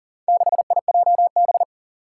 Morse code, heard as a keyed tone, about 37 wpm, 700 Hz. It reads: 6IJB
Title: Four Challenging Letter-Number Combinations 40wpm